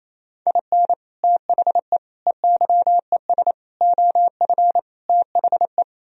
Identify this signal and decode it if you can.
Morse code, heard as a keyed tone, about 36 wpm, 700 Hz. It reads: IN THE EYES OF THE